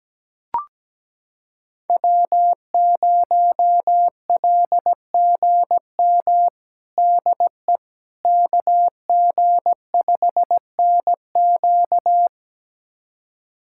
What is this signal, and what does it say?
Morse code, heard as a keyed tone, about 17 wpm, 700 Hz. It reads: W0LGM DE KG5NQ
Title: USA Callsign DE Callsign - 5 Characters - 17wpm